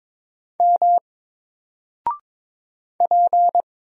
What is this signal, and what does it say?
Morse code, heard as a keyed tone, about 22 wpm, 700 Hz. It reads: M P
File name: Single-Letter-Number-22wpm